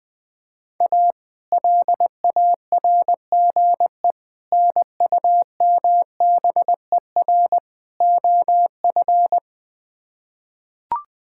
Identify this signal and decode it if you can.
Morse code, heard as a keyed tone, about 20 wpm, 700 Hz. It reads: A LARGE NUMBER OF
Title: Build A Phrase - 4-Word Phrases - 20wpm